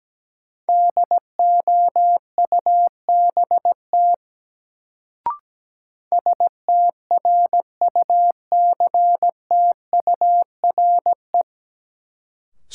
Morse code, heard as a keyed tone, about 17 wpm, 700 Hz. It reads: DOUBT STRUCTURE
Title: Sentences from Top 1500 Words - New Word Drill Alt - 17wpm